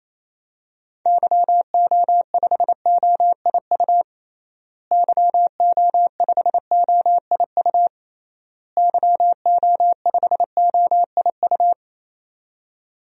Morse code, heard as a keyed tone, about 28 wpm, 700 Hz. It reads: YO5OIU YO5OIU YO5OIU
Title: DXCC-100 Call Signs - 6 Characters - Repeated 3x - 28wpm